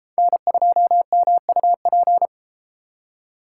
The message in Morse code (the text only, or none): N2MUP